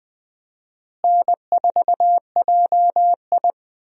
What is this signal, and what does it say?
Morse code, heard as a keyed tone, about 20 wpm, 700 Hz. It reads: N4JI
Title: USA Call Signs - 4 Characters - Repeated 2x - 20wpm